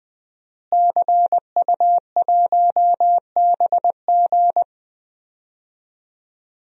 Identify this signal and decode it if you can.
Morse code, heard as a keyed tone, about 20 wpm, 700 Hz. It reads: CU1BG
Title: DXCC-100 Call Signs - 5 Characters - Repeated 2x - 20wpm